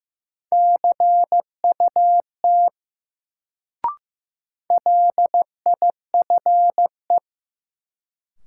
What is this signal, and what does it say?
Morse code, heard as a keyed tone, about 15 wpm, 700 Hz. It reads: CUT LIFE
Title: Sentences from Top 300 Words - New Word Drill - 15wpm